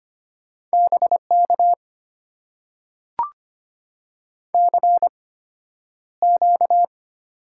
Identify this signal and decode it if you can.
Morse code, heard as a keyed tone, about 25 wpm, spaced as for 11 wpm, 700 Hz. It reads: BK CQ
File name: Top 90 QSO Elements - 8x Character Spacing - 25wpm